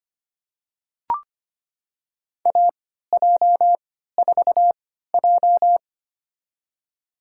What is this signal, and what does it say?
Morse code, heard as a keyed tone, about 25 wpm, spaced as for 14 wpm, 700 Hz. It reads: AJ4J
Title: USA Call Signs - 4 Characters - 3x Character Spacing - 25wpm